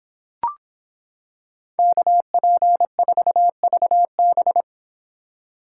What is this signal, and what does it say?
Morse code, heard as a keyed tone, about 26 wpm, 700 Hz. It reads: KP4VB